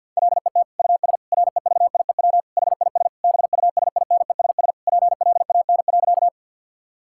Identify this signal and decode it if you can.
Morse code, heard as a keyed tone, about 50 wpm, 700 Hz. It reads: LET US REVIEW HIS BUSINESS PLAN.
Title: Sentences from Top 1000 Words - Review Part 6 - 50wpm